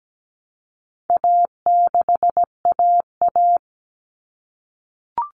Morse code, heard as a keyed tone, about 17 wpm, 700 Hz. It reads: A6AA
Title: All Cut Numbers - 4-Digit 17wpm